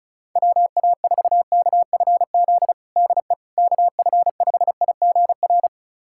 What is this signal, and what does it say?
Morse code, heard as a keyed tone, about 34 wpm, 700 Hz. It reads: WA4KFZ DE KF5IGR
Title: USA Callsign DE Callsign - 6 Characters - 35wpm